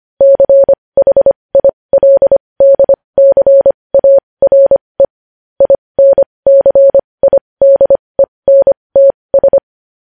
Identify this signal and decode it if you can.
Morse code, heard as a keyed tone, about 25 wpm, spaced as for 20 wpm, 550 Hz. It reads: CHILDCARE INCIDENTS
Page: news headlines in morse code at 20 wpm